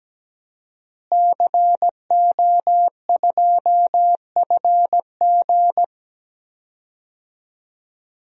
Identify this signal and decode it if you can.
Morse code, heard as a keyed tone, about 17 wpm, 700 Hz. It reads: CO2FG
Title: DXCC-100 Call Signs - 5 Characters - Repeated 2x - 17wpm